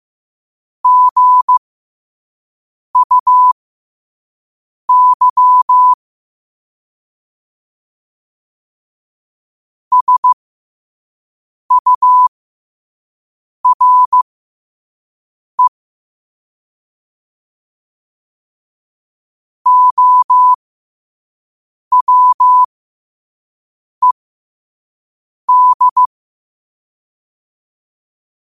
Morse code, heard as a keyed tone, about 15 wpm, spaced as for 5 wpm, 1000 Hz. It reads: GUY SURE OWED
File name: qotdfiles / 5wpm